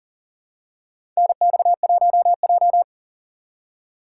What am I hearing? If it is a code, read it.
Morse code, heard as a keyed tone, about 38 wpm, 700 Hz. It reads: NX1J
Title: USA Call Signs - 4 Characters - 3x Character Spacing - 40wpm